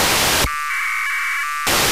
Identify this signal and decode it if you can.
Packet radio (AX.25, as in APRS).
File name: Horyu-4 10h28 UTC AFSK and CW JE9PEL